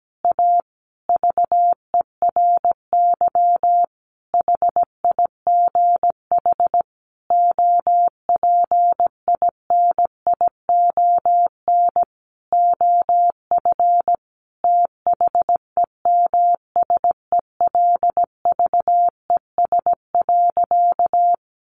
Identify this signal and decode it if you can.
Morse code, heard as a keyed tone, about 17 wpm, 700 Hz. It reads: A VERY HIGH OPINION OF THEMSELVES.
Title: Sentences from Top 1300 Words 17wpm